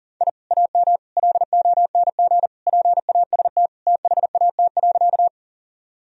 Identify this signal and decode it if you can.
Morse code, heard as a keyed tone, about 40 wpm, 700 Hz. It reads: I AM LONG PAST THAT.
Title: Sentences from Top 1000 Words - Review Part 3 - 40wpm